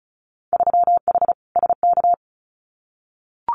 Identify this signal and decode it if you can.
Morse code, heard as a keyed tone, about 36 wpm, 700 Hz. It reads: 3H SK
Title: Winter Field Day Exchanges 35wpm